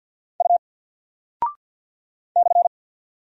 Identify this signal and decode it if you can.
Morse code, heard as a keyed tone, about 49 wpm, 700 Hz. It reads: U /
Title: Single Character (Letter:Number:Punctuation:Prosign) Alt 50wpm